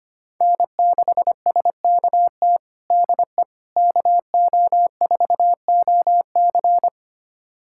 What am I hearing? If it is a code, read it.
Morse code, heard as a keyed tone, about 25 wpm, 700 Hz. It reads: N6SKT DE KO4OC
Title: USA Callsign DE Callsign - 5 Characters - 25wpm